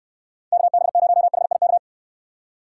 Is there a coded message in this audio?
Yes: Morse code from 0.5 to 1.8 s, heard as a keyed tone, about 69 wpm, 700 Hz. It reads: KB0HIK